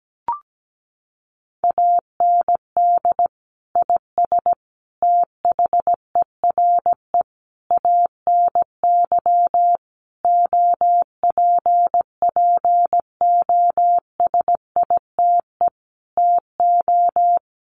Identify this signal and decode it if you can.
Morse code, heard as a keyed tone, about 17 wpm, 700 Hz. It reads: AND IS THERE ANY OPPOSITE TO
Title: Sentences from Top 1300 Words 17wpm